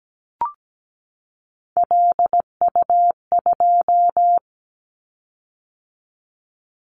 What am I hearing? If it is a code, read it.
Morse code, heard as a keyed tone, about 17 wpm, 700 Hz. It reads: LU2
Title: Callsign Prefixes for Top 100 DXCC - 17wpm